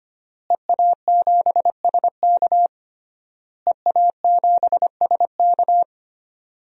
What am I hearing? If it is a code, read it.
Morse code, heard as a keyed tone, about 25 wpm, 700 Hz. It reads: EA7SK EA7SK
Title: DXCC-100 Call Signs - 5 Characters - Repeated 3x - 25wpm